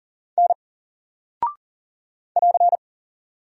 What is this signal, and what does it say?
Morse code, heard as a keyed tone, about 40 wpm, 700 Hz. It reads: N <AR>
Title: Single Character (Letter:Number:Punctuation:Prosign) Alt 40wpm